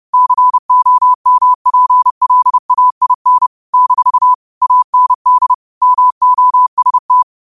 Morse code, heard as a keyed tone, about 30 wpm, 1000 Hz. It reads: COMPLAIN - AND MOST
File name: qotdfiles / 30wpm